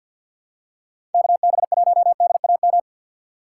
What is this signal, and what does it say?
Morse code, heard as a keyed tone, about 49 wpm, 700 Hz. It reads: KB1DAM
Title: USA Call Signs - 6 Characters - 4x Character Spacing - 50wpm